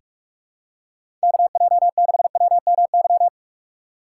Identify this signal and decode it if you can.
Morse code, heard as a keyed tone, about 46 wpm, 700 Hz. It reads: KJ6WMY
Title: USA Call Signs - 6 Characters - 2x Character Spacing - 45wpm